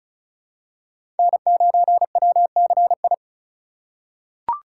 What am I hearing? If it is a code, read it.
Morse code, heard as a keyed tone, about 36 wpm, 700 Hz. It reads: N9WCI
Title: USA Call Signs - 3x Character Spacing - 35wpm